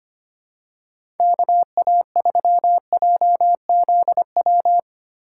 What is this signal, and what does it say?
Morse code, heard as a keyed tone, about 25 wpm, 700 Hz. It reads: KA3JZW
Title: USA Call Signs - 6 Characters - 3x Character Spacing - 25wpm